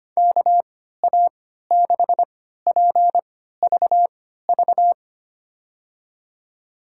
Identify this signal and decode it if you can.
Morse code, heard as a keyed tone, about 25 wpm, spaced as for 14 wpm, 700 Hz. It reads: KA6PVV
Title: USA Call Signs - 3x Character Spacing - 25wpm